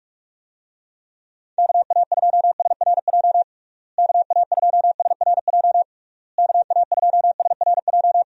Morse code, heard as a keyed tone, about 46 wpm, 700 Hz. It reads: KA2SRJ KA2SRJ KA2SRJ
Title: USA Call Signs - 6 Characters - Repeated 3x - 45wpm